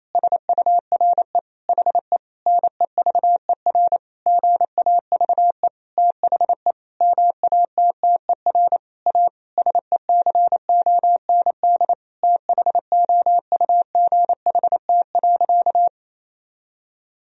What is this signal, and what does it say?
Morse code, heard as a keyed tone, about 28 wpm, 700 Hz. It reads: SURE HE NEVER GAVE THE MATTER A SECOND THOUGHT.
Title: Sentences from Top 1000 Words - Review Part 2 - 28wpm